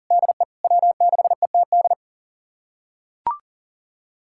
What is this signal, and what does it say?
Morse code, heard as a keyed tone, about 40 wpm, 700 Hz. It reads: DE W6ETD
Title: USA Callsign DE Callsign - 5 Characters - 40wpm